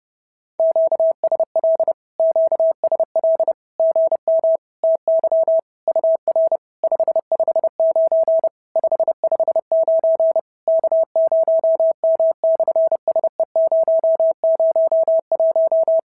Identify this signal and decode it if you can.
Morse code, heard as a keyed tone, about 30 wpm, 650 Hz. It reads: QSL QSL GM TY UR 559 559 K0M/SE001
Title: SOTA QSO Practice 30wpm